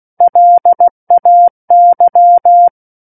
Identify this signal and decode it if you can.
Morse code, heard as a keyed tone, about 16 wpm, 700 Hz. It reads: LAY
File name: Book One / 07-Chapter 7 - 16 WPM